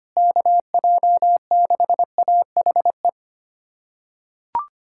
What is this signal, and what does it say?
Morse code, heard as a keyed tone, about 25 wpm, 700 Hz. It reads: KJ6AHE
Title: USA Callsign DE Callsign - 6 Characters - 25wpm